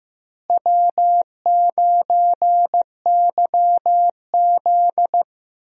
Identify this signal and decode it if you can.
Morse code, heard as a keyed tone, about 15 wpm, 700 Hz. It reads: W9YZ
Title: USA Call Signs - 4 Characters - Repeated 3x - 15wpm